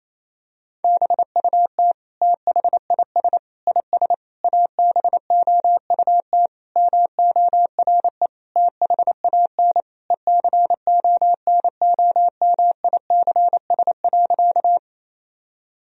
Morse code, heard as a keyed tone, about 28 wpm, 700 Hz. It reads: BUT THIS IS ABOUT MORE THAN ECONOMICS.
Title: Sentences from Top 1000 Words - Review Part 3 - 28wpm